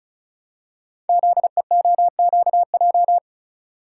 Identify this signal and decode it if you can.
Morse code, heard as a keyed tone, about 36 wpm, 700 Hz. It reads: ZEOQJ
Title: Code Groups - 5 Letters - 35wpm